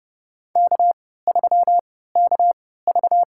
Morse code, heard as a keyed tone, about 30 wpm, spaced as for 17 wpm, 700 Hz. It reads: K3KV
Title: USA Call Signs - 4 Characters - 3x Character Spacing - 30wpm